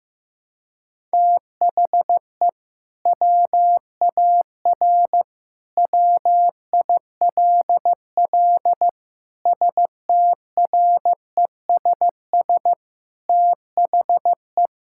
Morse code, heard as a keyed tone, about 16 wpm, 700 Hz. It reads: THE WAR WILL STRESS THE
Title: Sentences from Top 1000 Words - Review Part 6 - 15wpm